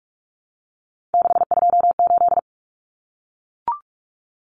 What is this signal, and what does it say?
Morse code, heard as a keyed tone, about 44 wpm, 700 Hz. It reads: 628
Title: Common Cut Numbers - 3-Digit 45wpm